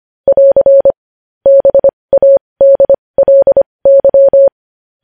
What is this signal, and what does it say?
Morse code, heard as a keyed tone, about 25 wpm, spaced as for 20 wpm, 550 Hz. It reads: <AR> BADLY
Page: news-headlines-in-morse-code-at-20-wpm